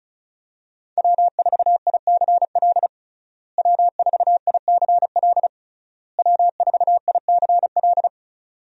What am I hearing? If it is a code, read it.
Morse code, heard as a keyed tone, about 36 wpm, 700 Hz. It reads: W4ICL W4ICL W4ICL
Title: USA Call Signs - 5 Characters - Repeated 3x - 35wpm